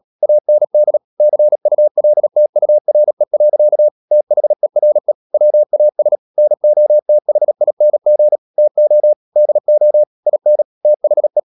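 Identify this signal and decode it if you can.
Morse code, heard as a keyed tone, about 37 wpm, 600 Hz. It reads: AND CULTURE. THERE WAS NOTHING TO DO IN THE
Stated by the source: Morse code practice at 35–39 wpm sending English text of Ken Follett's novel Never, from CW med Gnister.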